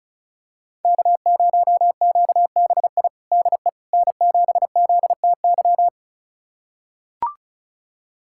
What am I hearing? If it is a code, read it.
Morse code, heard as a keyed tone, about 34 wpm, 700 Hz. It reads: K0QBI DE N7ZTY
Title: USA Callsign DE Callsign - 5 Characters - 35wpm